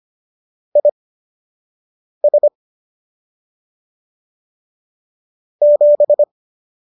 Morse code, heard as a keyed tone, about 25 wpm, spaced as for 6 wpm, 590 Hz. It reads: IS 7
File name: qso002 06